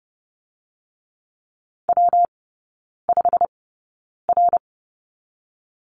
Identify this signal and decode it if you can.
Morse code, heard as a keyed tone, about 30 wpm, 700 Hz. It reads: W 5 R
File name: Lesson 19e - Letter w - Three Characters 30wpm